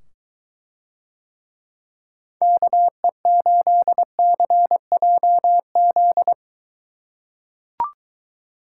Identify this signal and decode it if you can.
Morse code, heard as a keyed tone, about 22 wpm, 700 Hz. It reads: KE8CJZ